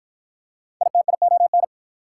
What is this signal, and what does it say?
Morse code, heard as a keyed tone, about 46 wpm, 690 Hz. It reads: ITION